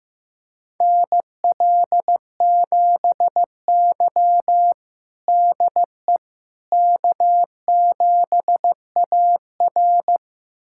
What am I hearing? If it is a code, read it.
Morse code, heard as a keyed tone, about 15 wpm, 700 Hz. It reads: NL7Y DE K7AR
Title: USA Callsign DE Callsign - 4 Characters - 15wpm